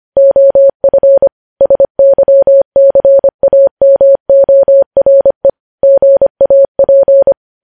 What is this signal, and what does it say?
Morse code, heard as a keyed tone, about 25 wpm, 550 Hz. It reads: OF SYCAMORE GAP
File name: podcast / News headlines-1x-25x25